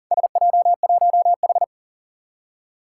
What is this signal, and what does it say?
Morse code, heard as a keyed tone, about 40 wpm, 700 Hz. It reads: SJ1H
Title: Four Challenging Letter-Number Combinations 40wpm